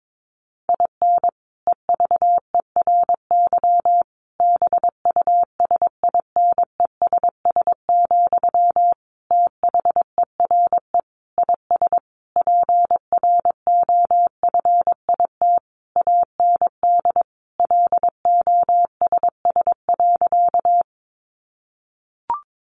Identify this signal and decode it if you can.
Morse code, heard as a keyed tone, about 22 wpm, 700 Hz. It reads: IN EVERY BUSINESS, THERE IS PROFIT AND LOSS.
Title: Sentences from Top 1400 Words 22wpm